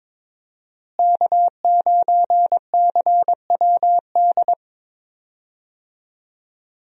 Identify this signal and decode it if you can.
Morse code, heard as a keyed tone, about 22 wpm, 700 Hz. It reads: K9CWD
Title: USA Call Signs - 5 Characters - Repeated 2x - 22wpm